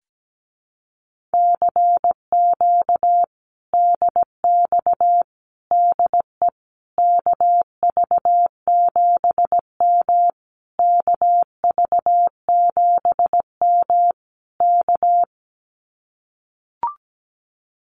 Morse code, heard as a keyed tone, about 17 wpm, 700 Hz. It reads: CQ DX DE KV7M KV7M K
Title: Common QSO Phrases Encore 17wpm